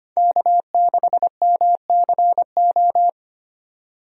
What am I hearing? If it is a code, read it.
Morse code, heard as a keyed tone, about 25 wpm, 700 Hz. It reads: K6MCO
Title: USA Call Signs - 5 Characters - 25wpm